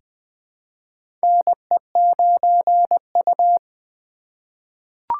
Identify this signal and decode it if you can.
Morse code, heard as a keyed tone, about 20 wpm, 700 Hz. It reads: NE9U
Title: K1USN SST QSO Practice - 20wpm